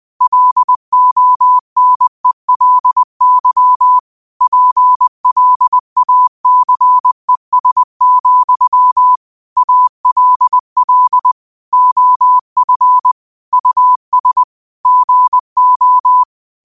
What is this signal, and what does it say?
Morse code, heard as a keyed tone, about 20 wpm, 1000 Hz. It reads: LONELY PLACES, ALL OF US GO